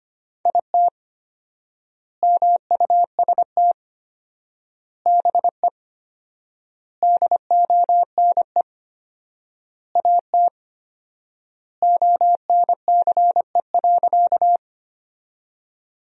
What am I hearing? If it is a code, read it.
Morse code, heard as a keyed tone, about 25 wpm, 700 Hz. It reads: IT MUST BE DONE AT ONCE.
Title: 4x Word-Spacing - Sentences from Top 500 Words - 25wpm